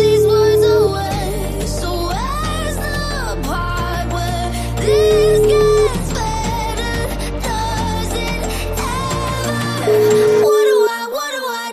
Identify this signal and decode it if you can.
Ringback tone (Europe, 425 Hz), with noise.